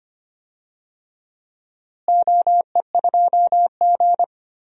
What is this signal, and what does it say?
Morse code, heard as a keyed tone, about 25 wpm, 700 Hz. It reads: OE2G
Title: DXCC-100 Call Signs - 4 Characters - Repeated 3x - 25wpm